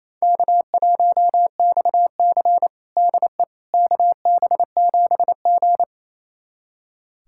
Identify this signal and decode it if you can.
Morse code, heard as a keyed tone, about 28 wpm, 700 Hz. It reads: K1XC DE KB7G
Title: USA Callsign DE Callsign - 4 Characters - 28wpm